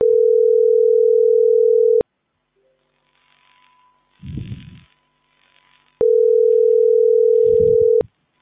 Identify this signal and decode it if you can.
Ringback tone (North America), 440 + 480 Hz.